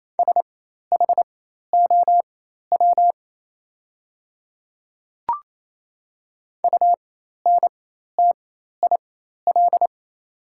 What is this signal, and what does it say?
Morse code, heard as a keyed tone, about 28 wpm, spaced as for 13 wpm, 700 Hz. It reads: SHOW UNTIL
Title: 4x Character Spacing - Top 300 Words - 28wpm